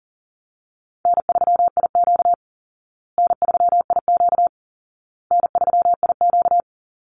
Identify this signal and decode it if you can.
Morse code, heard as a keyed tone, about 40 wpm, 700 Hz. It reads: N3IQ N3IQ N3IQ
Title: Call Signs Repeated 2x - 40wpm